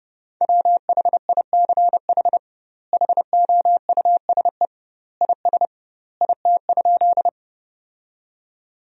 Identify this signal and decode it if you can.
Morse code, heard as a keyed tone, about 30 wpm, 700 Hz. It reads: WHICH HOUSE IS IT?
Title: 2x Word-Spacing - Sentences from Top 200 Words - 30wpm